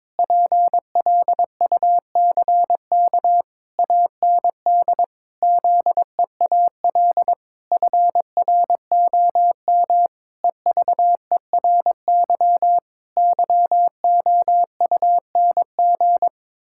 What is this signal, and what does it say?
Morse code, heard as a keyed tone, about 22 wpm, 700 Hz. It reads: PLUCK AND ZEAL FROM EVERY YOUNG
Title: Pangram Challenge v2 - 22wpm